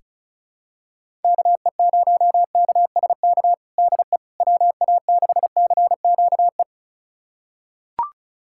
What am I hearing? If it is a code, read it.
Morse code, heard as a keyed tone, about 34 wpm, 700 Hz. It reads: KE0KSK DE WA6CQE
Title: USA Callsign DE Callsign 35wpm